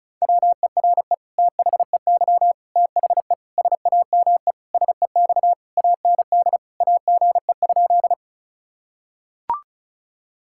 Morse code, heard as a keyed tone, about 34 wpm, 700 Hz. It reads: WERE THEY THE SAME SEX AND AGE?
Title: Sentences from Top 1600 Words 35wpm